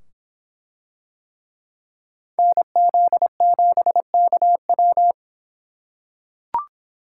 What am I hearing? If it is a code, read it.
Morse code, heard as a keyed tone, about 24 wpm, 700 Hz. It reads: NZ7KW